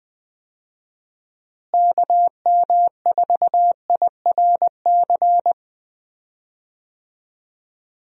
Morse code, heard as a keyed tone, about 21 wpm, 700 Hz. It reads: KM4IRC
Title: USA Call Signs - 6 Characters - Repeated 3x - 20wpm